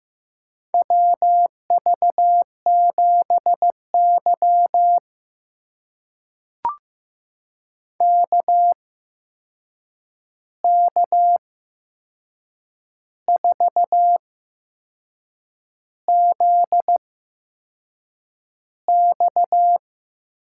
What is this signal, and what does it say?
Morse code, heard as a keyed tone, about 15 wpm, spaced as for 6 wpm, 700 Hz. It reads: WV7Y KK4ZX
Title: USA Call Signs - 8x Character Spacing - 15wpm